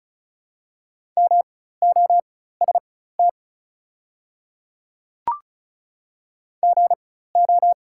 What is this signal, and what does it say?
Morse code, heard as a keyed tone, about 36 wpm, spaced as for 16 wpm, 700 Hz. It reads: MOST GO
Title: 4x Character Spacing - Top 100 Words - 35wpm